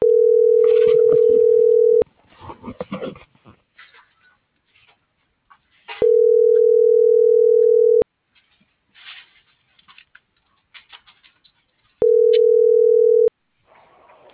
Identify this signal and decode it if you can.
Ringback tone (North America), 440 + 480 Hz.